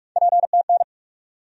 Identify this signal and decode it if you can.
Morse code, heard as a keyed tone, about 47 wpm, 700 Hz. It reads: PTN
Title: USA Call Sign Postfixes - 3 Letters 45wpm